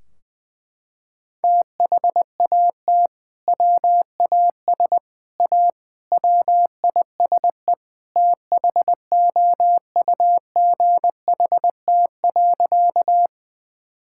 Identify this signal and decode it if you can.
Morse code, heard as a keyed tone, about 20 wpm, 700 Hz. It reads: THAT WAS A WISE THOUGHT.
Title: Sentences from Top 1600 Words 20wpm